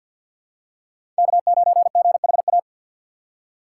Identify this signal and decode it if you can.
Morse code, heard as a keyed tone, about 47 wpm, 700 Hz. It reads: K9GHA
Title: USA Call Signs - 5 Characters - 3x Character Spacing - 50wpm